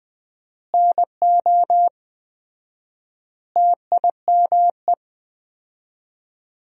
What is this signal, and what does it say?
Morse code, heard as a keyed tone, about 20 wpm, 700 Hz. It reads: NO TIME
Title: 4x Word-Spacing - Sentences from Top 400 Words - 20wpm